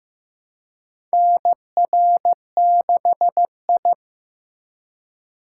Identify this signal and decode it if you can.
Morse code, heard as a keyed tone, about 15 wpm, 700 Hz. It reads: NR6I
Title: USA Call Signs - 4 Characters - 3x Character Spacing - 15wpm